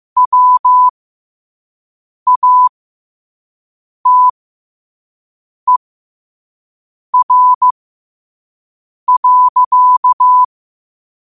Morse code, heard as a keyed tone, about 15 wpm, spaced as for 5 wpm, 1000 Hz. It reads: WATER.
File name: qotdfiles / 5wpm